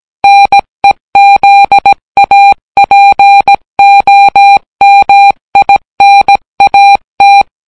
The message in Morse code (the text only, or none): NEZAPOMINAT